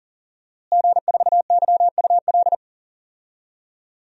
Morse code, heard as a keyed tone, about 40 wpm, 700 Hz. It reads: G4YUL
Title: DXCC-100 Call Signs - 5 Characters - Repeated 3x - 40wpm